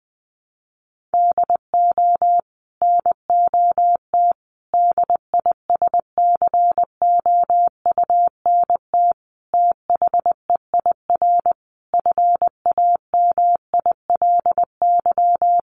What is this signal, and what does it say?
Morse code, heard as a keyed tone, about 19 wpm, 700 Hz. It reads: DO NOT DISCOUNT THEIR FAMILY
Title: Sentences from Top 1200 Words 20wpm